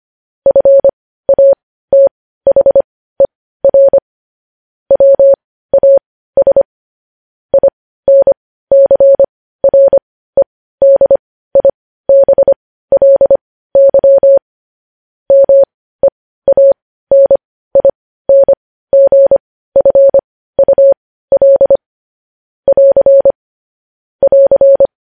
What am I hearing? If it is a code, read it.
Morse code, heard as a keyed tone, about 25 wpm, spaced as for 15 wpm, 550 Hz. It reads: FATHER WAS INCREDIBLY MEANINGFUL <AR> <AR>
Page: news headlines in morse code at 15 wpm